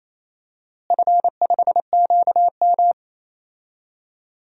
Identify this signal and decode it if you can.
Morse code, heard as a keyed tone, about 28 wpm, 700 Hz. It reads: F5QM
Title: DXCC-100 Call Signs - 4 Characters - Repeated 3x - 28wpm